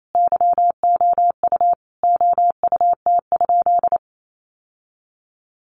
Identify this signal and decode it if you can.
Morse code, heard as a keyed tone, about 28 wpm, 700 Hz. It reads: YOU OUT?
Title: Sentences from Top 1300 Words 28wpm